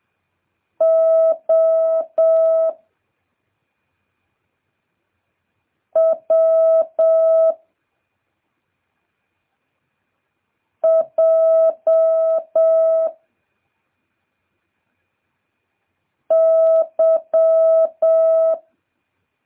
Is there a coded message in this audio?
Yes: Morse code from 0.8 to 18.5 s, heard as a keyed tone, about 7 wpm, 650 Hz. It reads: O W J Y